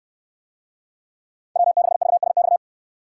Morse code, heard as a keyed tone, about 69 wpm, 700 Hz. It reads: W6VSX